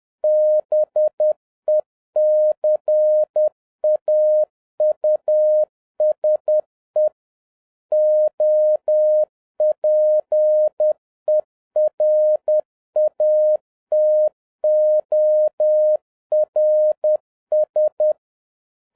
Morse code, heard as a keyed tone, about 10 wpm, 610 Hz. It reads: BECAUSE OPERATORS
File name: 2011-07-mp3 / n21 GREETINGS FROM SINGAPORE 10wpm